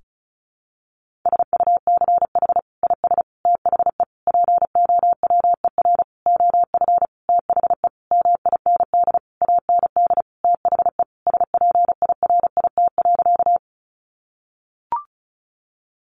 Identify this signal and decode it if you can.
Morse code, heard as a keyed tone, about 34 wpm, 700 Hz. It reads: SUCH IS THE POWER OF THE MIND AND THE SPIRIT.
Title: Sentences from Top 1400 Words 35wpm